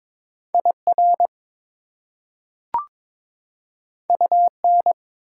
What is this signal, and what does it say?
Morse code, heard as a keyed tone, about 21 wpm, 700 Hz. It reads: IR UN
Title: Prefixes and Suffixes 22wpm